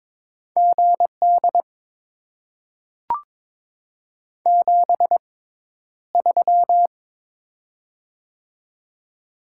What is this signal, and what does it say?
Morse code, heard as a keyed tone, about 22 wpm, spaced as for 11 wpm, 700 Hz. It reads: GD 73
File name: Top 90 QSO Elements - 6x Character Spacing - 22wpm